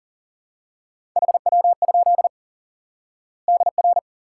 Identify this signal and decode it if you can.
Morse code, heard as a keyed tone, about 37 wpm, 700 Hz. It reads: HW? DR
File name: IQE Course 3 - Lesson 89 - YAGI - Two Elements - 40wpm